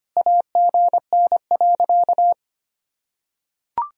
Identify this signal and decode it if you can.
Morse code, heard as a keyed tone, about 25 wpm, 700 Hz. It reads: AGN.
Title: Sentences from Top 300 Words and Abbreviations 25wpm